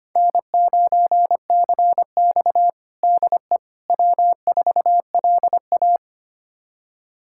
Morse code, heard as a keyed tone, about 25 wpm, 700 Hz. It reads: N9CX DE W4LA
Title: USA Callsign DE Callsign - 4 Characters - 25wpm